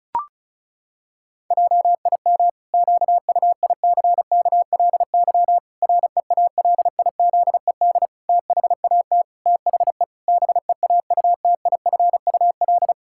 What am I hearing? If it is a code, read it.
Morse code, heard as a keyed tone, about 34 wpm, 700 Hz. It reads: JIM QUICKLY REALIZED THAT THE BEAUTIFUL
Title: Pangram Challenge v2 - 35wpm